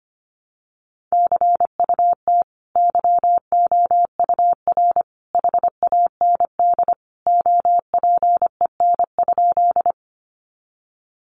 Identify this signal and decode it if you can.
Morse code, heard as a keyed tone, about 25 wpm, 700 Hz. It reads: CUT YOUR HAND OPEN?
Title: Sentences from Top 300 Words 25wpm